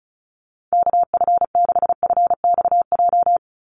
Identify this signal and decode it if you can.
Morse code, heard as a keyed tone, about 34 wpm, 700 Hz. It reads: KF6FXJ
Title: USA Call Signs 35wpm